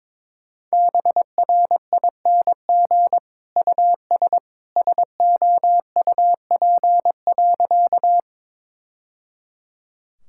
Morse code, heard as a keyed tone, about 22 wpm, 700 Hz. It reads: BRING US SOUP.
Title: Sentences from Top 1000 Words - Review Part 6 - 22wpm